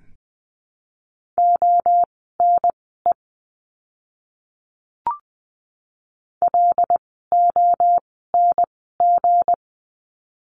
Morse code, heard as a keyed tone, about 20 wpm, spaced as for 15 wpm, 700 Hz. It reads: ONE LONG
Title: 160 Lesson 26f - Letter g - Words 20wpm